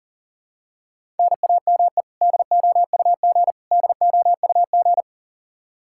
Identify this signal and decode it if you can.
Morse code, heard as a keyed tone, about 37 wpm, 700 Hz. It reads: NAME DOUG DOUG
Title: Practice QSO - Exchange 1 - Beginning - 40wpm